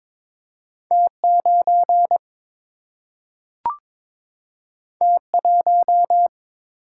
Morse code, heard as a keyed tone, about 22 wpm, 700 Hz. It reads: T9 T1
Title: Single-Letter Single-Number 22wpm